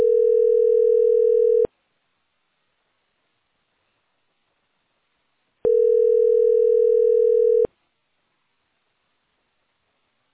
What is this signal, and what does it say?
Ringback tone (North America), 440 + 480 Hz.